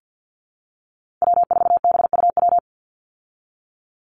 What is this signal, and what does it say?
Morse code, heard as a keyed tone, about 50 wpm, 700 Hz. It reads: W4BUW
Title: Call Signs Repeated 2x - 50wpm